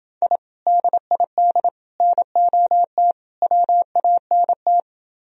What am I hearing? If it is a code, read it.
Morse code, heard as a keyed tone, about 28 wpm, 700 Hz. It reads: I DID NOT WANT